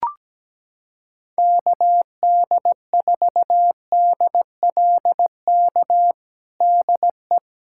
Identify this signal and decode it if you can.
Morse code, heard as a keyed tone, about 17 wpm, 700 Hz. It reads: KD4DLK DE
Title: USA Callsign DE Callsign - 6 Characters - 17wpm